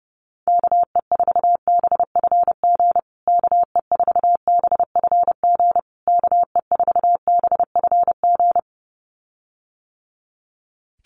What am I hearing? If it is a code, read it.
Morse code, heard as a keyed tone, about 30 wpm, 700 Hz. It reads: KE4BFG KE4BFG KE4BFG
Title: USA Call Signs Repeated 3x 30wpm